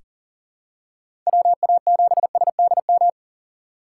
Morse code, heard as a keyed tone, about 40 wpm, 700 Hz. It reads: WA7SDM
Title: USA Call Signs - 6 Characters - 3x Character Spacing - 40wpm